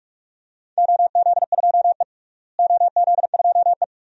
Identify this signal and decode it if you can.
Morse code, heard as a keyed tone, about 47 wpm, 700 Hz. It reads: OZ2E OZ2E
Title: DXCC-100 Call Signs - 4 Characters - Repeated 2x - 45wpm